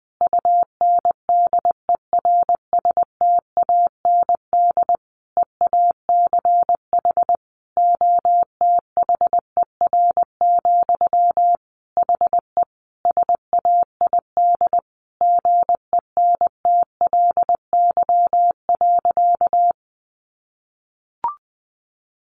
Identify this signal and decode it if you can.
Morse code, heard as a keyed tone, about 20 wpm, 700 Hz. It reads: UNDERSTAND EACH OTHER, HE SAID GENTLY.
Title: Sentences from Top 1500 Words 20wpm